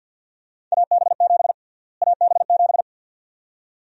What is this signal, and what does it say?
Morse code, heard as a keyed tone, about 48 wpm, 700 Hz. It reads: AB7 AB7
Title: USA Call Sign Prefixes - Repeated 2x - 50wpm